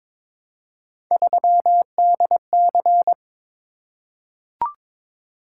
Morse code, heard as a keyed tone, about 22 wpm, 700 Hz. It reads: 3DC
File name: practice / 22wpm / Three Letter-Number Combinations - 22wpm